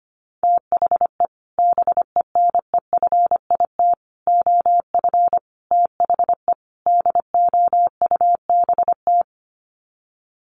Morse code, heard as a keyed tone, about 25 wpm, 700 Hz. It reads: THE BENEFIT OF THE DOUBT
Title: Sets of 5 Words Encore 25wpm